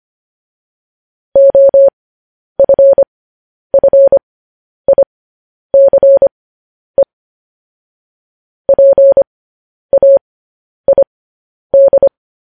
Morse code, heard as a keyed tone, about 25 wpm, spaced as for 10 wpm, 550 Hz. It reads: OFFICE PAID
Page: news headlines in morse code at 10 wpm